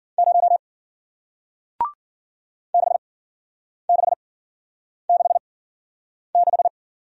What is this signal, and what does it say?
Morse code, heard as a keyed tone, about 51 wpm, spaced as for 9 wpm, 700 Hz. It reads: 0 6666